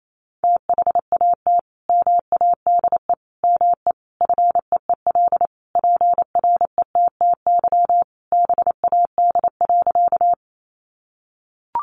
Morse code, heard as a keyed tone, about 28 wpm, 700 Hz. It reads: THAT MADE ME FEEL PRETTY BAD.
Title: Sentences from Top 1000 Words 28wpm